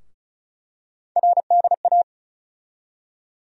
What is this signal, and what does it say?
Morse code, heard as a keyed tone, about 34 wpm, 700 Hz. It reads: RDA
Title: USA Call Sign Postfixes - 3 Letters 35wpm